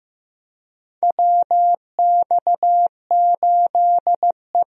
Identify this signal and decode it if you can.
Morse code, heard as a keyed tone, about 15 wpm, 700 Hz. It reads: WX8E
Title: USA Call Signs - 4 Characters - Repeated 2x - 15wpm